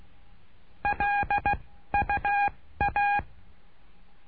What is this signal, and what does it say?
Morse code, heard as a keyed tone, about 16 wpm, spaced as for 13 wpm, 810 Hz. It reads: LUA
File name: cursos / CURSO DE TELEGRAFIA / Telegrafia - Trilha 09